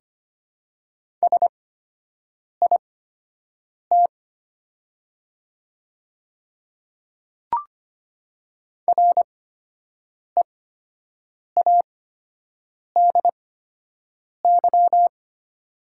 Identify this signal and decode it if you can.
Morse code, heard as a keyed tone, about 25 wpm, spaced as for 7 wpm, 690 Hz. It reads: SIT READY
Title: 8x Character Spacing - Top 500 Words - 25wpm